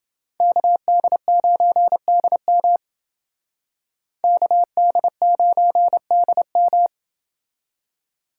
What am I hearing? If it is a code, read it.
Morse code, heard as a keyed tone, about 27 wpm, 700 Hz. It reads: KD9DM KD9DM